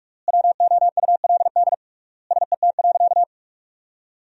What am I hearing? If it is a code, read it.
Morse code, heard as a keyed tone, about 41 wpm, 700 Hz. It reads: WOULD SET.
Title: 3x Word-Spacing - Sentences from Top 400 Words - 45wpm